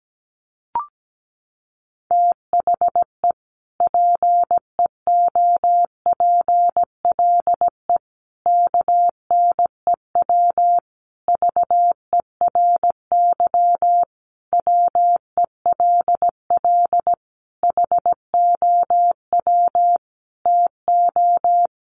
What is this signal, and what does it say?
Morse code, heard as a keyed tone, about 17 wpm, 700 Hz. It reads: THE PEOPLE KNEW VERY WELL HOW TO
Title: Sentences from Top 1100 Words 17wpm